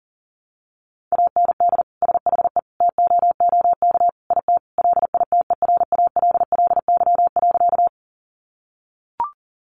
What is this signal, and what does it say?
Morse code, heard as a keyed tone, about 40 wpm, 700 Hz. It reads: AND SHE TOOK IT LITERALLY.
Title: Sentences from Top 1400 Words 40wpm